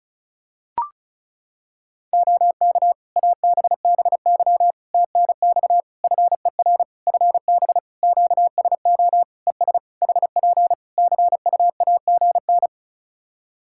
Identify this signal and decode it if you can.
Morse code, heard as a keyed tone, about 34 wpm, 700 Hz. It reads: OK ABBY TNX FER FB QSO ES HP CUAGN
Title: Practice QSO - Exchange 4 - Ending - 35wpm